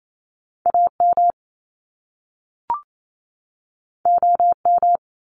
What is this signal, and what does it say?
Morse code, heard as a keyed tone, about 28 wpm, 700 Hz. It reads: AM OM
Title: Top Two Letter Combinations in Words 28wpm